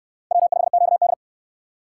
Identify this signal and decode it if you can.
Morse code, heard as a keyed tone, about 63 wpm, 700 Hz. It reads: W5QD